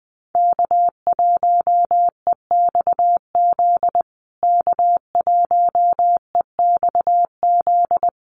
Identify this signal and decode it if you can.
Morse code, heard as a keyed tone, about 20 wpm, 700 Hz. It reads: K1EXZ K1EXZ
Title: USA Call Signs Repeated 2x 20wpm